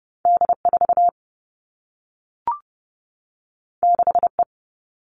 Morse code, heard as a keyed tone, about 30 wpm, 700 Hz. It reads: D4 6E
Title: All Cut Numbers - 2-Digit 30wpm